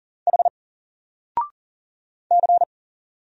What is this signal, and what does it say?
Morse code, heard as a keyed tone, about 40 wpm, 700 Hz. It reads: H C
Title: Single Letter-Number 40wpm